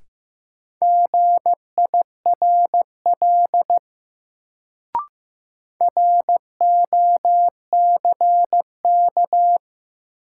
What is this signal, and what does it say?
Morse code, heard as a keyed tone, about 15 wpm, 700 Hz. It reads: GIRL ROCK
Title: Sentences from Top 400 Words - New Word Drill - 15wpm